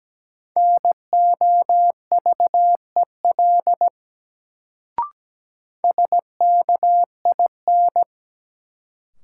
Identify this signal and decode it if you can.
Morse code, heard as a keyed tone, about 17 wpm, 700 Hz. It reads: NOVEL SKIN
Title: Sentences from Top 1300 Words - New Word Drill - 17wpm